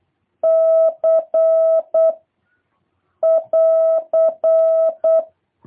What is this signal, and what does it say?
Morse code, heard as a keyed tone, about 8 wpm, 650 Hz. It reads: C <AR>